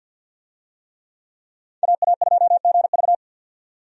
Morse code, heard as a keyed tone, about 51 wpm, 700 Hz. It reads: AA2GV
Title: USA Call Signs - 5 Characters - 50wpm